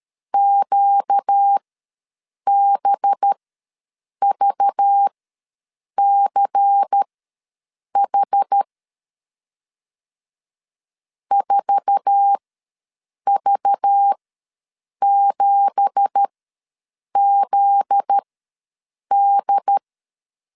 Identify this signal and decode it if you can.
Morse code, heard as a keyed tone, about 13 wpm, spaced as for 7 wpm, 800 Hz. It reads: QBVCH 4V7ZD
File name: LEZIONI / 9 2867 ws 12wpm 3431